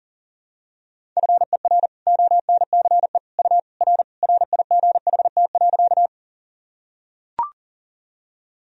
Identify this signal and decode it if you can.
Morse code, heard as a keyed tone, about 40 wpm, 700 Hz. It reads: FER ONCE U R RIGHT.
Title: Sentences from Top 300 Words and Abbreviations 40wpm